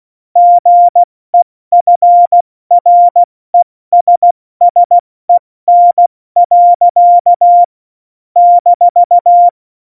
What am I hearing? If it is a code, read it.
Morse code, heard as a keyed tone, about 16 wpm, 700 Hz. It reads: GEFRESSEN. -
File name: Morsekurs / Witze / Witz 22. August 17, Tempo 70 F80